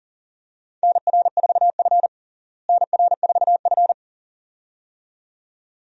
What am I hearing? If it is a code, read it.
Morse code, heard as a keyed tone, about 40 wpm, 700 Hz. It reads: NR4F NR4F
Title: USA Call Signs - 4 Characters - Repeated 2x - 40wpm